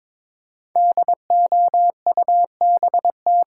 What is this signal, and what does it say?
Morse code, heard as a keyed tone, about 22 wpm, 690 Hz. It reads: DOUBT
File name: 5 Letter Words - First 3 Letters Given - 22wpm